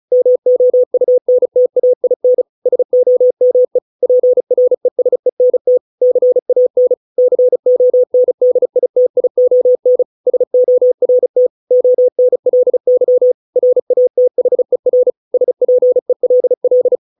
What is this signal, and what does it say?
Morse code, heard as a keyed tone, about 34 wpm, 500 Hz. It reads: MOUNTAIN SOME PRESENT CAN CONDITION SORT ONLY RATHER SPELL